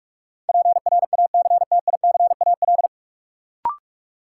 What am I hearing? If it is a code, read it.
Morse code, heard as a keyed tone, about 46 wpm, 700 Hz. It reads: PRACTICAL
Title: Sentences from Top 1400 Words - New Word Drill - 45wpm